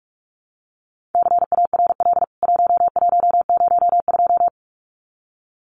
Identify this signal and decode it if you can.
Morse code, heard as a keyed tone, about 44 wpm, 700 Hz. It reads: CARL 1102
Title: CWT Exchange 45wpm